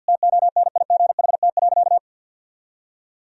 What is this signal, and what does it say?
Morse code, heard as a keyed tone, about 50 wpm, 700 Hz. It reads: TONIGHT.
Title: Sentences from Top 1000 Words - Review Part 3 - 50wpm